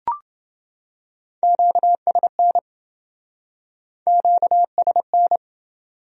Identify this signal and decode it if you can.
Morse code, heard as a keyed tone, about 28 wpm, 700 Hz. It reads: QSN QSN